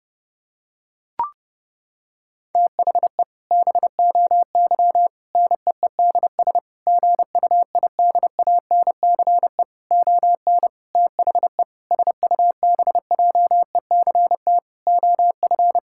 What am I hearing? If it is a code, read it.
Morse code, heard as a keyed tone, about 30 wpm, 700 Hz. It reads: THE BOY NEEDS GUIDANCE ON THE SUBJECT OF
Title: Sentences from Top 1600 Words 30wpm